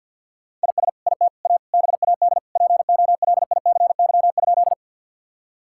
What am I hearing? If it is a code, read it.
Morse code, heard as a keyed tone, about 50 wpm, 700 Hz. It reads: IS IT A BAD POLICY?
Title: Sentences from Top 1000 Words - Review Part 5 - 50wpm